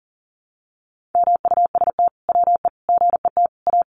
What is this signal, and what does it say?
Morse code, heard as a keyed tone, about 40 wpm, 700 Hz. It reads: MUST WE GET A
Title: Sentences from Top 1400 Words 40wpm